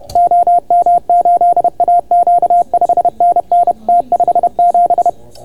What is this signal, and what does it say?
Morse code, heard as a keyed tone, about 30 wpm, 690 Hz. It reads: OM8AQ 5NN T57